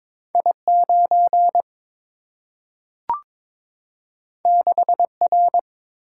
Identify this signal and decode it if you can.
Morse code, heard as a keyed tone, about 22 wpm, 700 Hz. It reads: I9 6R
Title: Single-Letter Single-Number 22wpm